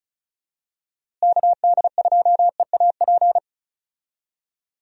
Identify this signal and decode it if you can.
Morse code, heard as a keyed tone, about 36 wpm, 700 Hz. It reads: KD2EAP
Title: USA Call Signs - 6 Characters - Repeated 2x - 35wpm